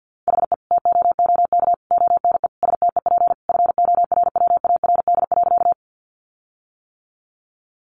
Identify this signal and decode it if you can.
Morse code, heard as a keyed tone, about 50 wpm, 700 Hz. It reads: HE TOOK ONE STEP FORWARD.
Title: Sentences from Top 1300 Words 50wpm